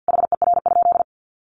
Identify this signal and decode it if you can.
Morse code, heard as a keyed tone, about 50 wpm, 700 Hz. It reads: HER?
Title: Sentences from Top 900 Words 50wpm